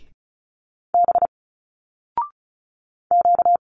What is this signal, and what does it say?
Morse code, heard as a keyed tone, about 36 wpm, 700 Hz. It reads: B Q
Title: Single Letters 35wpm